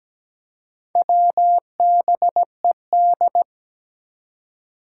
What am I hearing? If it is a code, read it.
Morse code, heard as a keyed tone, about 17 wpm, 700 Hz. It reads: WBED
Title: Four Challenging Letter-Number Combinations 17wpm